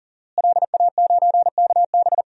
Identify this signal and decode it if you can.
Morse code, heard as a keyed tone, about 40 wpm, 700 Hz. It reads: LA9KB
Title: DXCC-100 Call Signs - 5 Characters - Repeated 3x - 40wpm